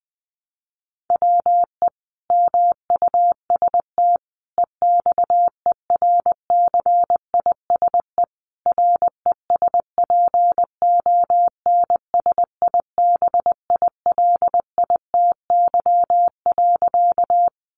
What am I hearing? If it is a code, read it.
Morse code, heard as a keyed tone, about 20 wpm, 700 Hz. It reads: WE MUST EXERCISE RESPONSIBILITY.
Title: Sentences from Top 1200 Words 20wpm